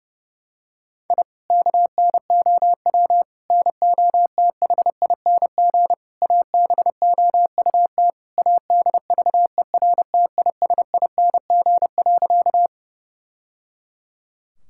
Morse code, heard as a keyed tone, about 30 wpm, 700 Hz. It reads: I KNOW NOTHING ABOUT ADVERTISING.
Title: Sentences from Top 1000 Words - Review Part 6 - 30wpm